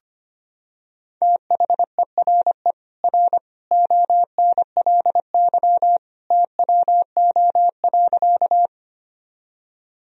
Morse code, heard as a keyed tone, about 25 wpm, 700 Hz. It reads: THERE R ONLY TWO.
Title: Sentences from Top 200 Words and Abbreviations 25wpm